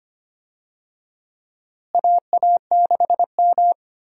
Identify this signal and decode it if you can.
Morse code, heard as a keyed tone, about 25 wpm, 700 Hz. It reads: AA6M
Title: USA Call Signs - 4 Characters - Repeated 3x - 25wpm